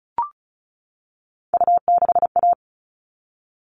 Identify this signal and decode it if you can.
Morse code, heard as a keyed tone, about 34 wpm, 700 Hz. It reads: U6A
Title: All Cut Numbers - 3-Digit 35wpm